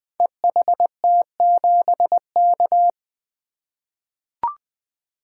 Morse code, heard as a keyed tone, about 20 wpm, 700 Hz. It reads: EHT7K
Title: Code Groups - 5 Letter:Numbers - 20wpm